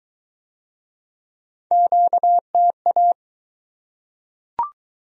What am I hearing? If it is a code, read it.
Morse code, heard as a keyed tone, about 21 wpm, 700 Hz. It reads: QTA